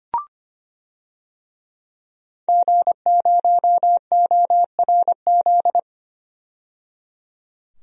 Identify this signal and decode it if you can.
Morse code, heard as a keyed tone, about 25 wpm, 700 Hz. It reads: G0ORZ
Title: DXCC-100 Call Signs - 5 Characters - 25wpm